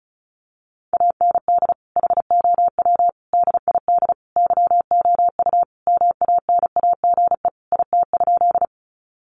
Morse code, heard as a keyed tone, about 34 wpm, 700 Hz. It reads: AND HOW DID YOU MANAGE IT?
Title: Sentences from Top 1400 Words 35wpm